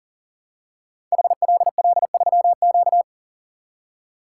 Morse code, heard as a keyed tone, about 42 wpm, 700 Hz. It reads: HLL3Q
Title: Code Groups - 5 Letter:Numbers - 40wpm